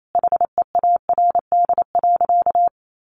Morse code, heard as a keyed tone, about 28 wpm, 700 Hz. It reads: HEARD.
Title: Sentences from Top 1100 Words 28wpm